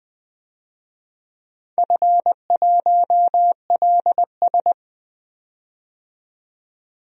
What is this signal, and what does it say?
Morse code, heard as a keyed tone, about 20 wpm, 700 Hz. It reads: F1LS
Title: DXCC-100 Call Signs - 4 Characters - 20wpm